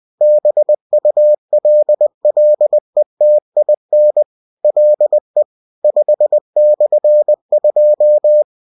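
Morse code, heard as a keyed tone, about 20 wpm, 600 Hz. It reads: BULLETIN LE 5/2
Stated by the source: Bulletin sent in CW.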